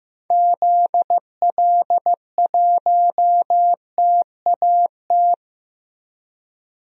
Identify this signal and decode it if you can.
Morse code, heard as a keyed tone, about 15 wpm, 700 Hz. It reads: ZL1TAT
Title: DXCC-100 Call Signs - 6 Characters - Repeated 3x - 15wpm